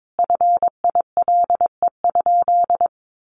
Morse code, heard as a keyed tone, about 22 wpm, 700 Hz. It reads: FILE?
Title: Sentences from Top 1200 Words 22wpm